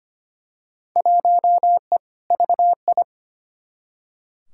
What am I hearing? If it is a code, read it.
Morse code, heard as a keyed tone, about 25 wpm, 700 Hz. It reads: 1E VI
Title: Field Day Exchanges Encore - 25wpm